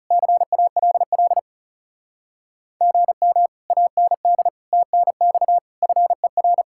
Morse code, heard as a keyed tone, about 37 wpm, 700 Hz. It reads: CALL GM AND TNX FER
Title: Speed-Racing Common QSO Phrases - 30wpm